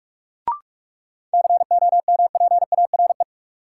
Morse code, heard as a keyed tone, about 44 wpm, 700 Hz. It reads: COMPARE
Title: Sentences from Top 1300 Words - New Word Drill Alt - 45wpm